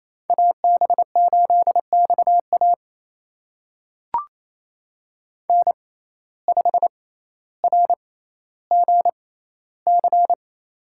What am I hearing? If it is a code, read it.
Morse code, heard as a keyed tone, about 28 wpm, spaced as for 14 wpm, 700 Hz. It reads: AB8XA N5RGC
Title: USA Call Signs - 5 Characters - 6x Character Spacing - 28wpm